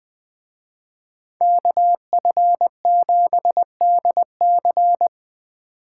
Morse code, heard as a keyed tone, about 20 wpm, 700 Hz. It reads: KF7DC